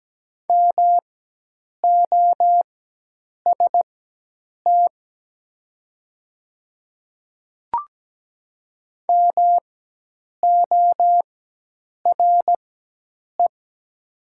Morse code, heard as a keyed tone, about 17 wpm, spaced as for 8 wpm, 700 Hz. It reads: MOST MORE
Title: 4x Character Spacing - Top 300 Words - 17wpm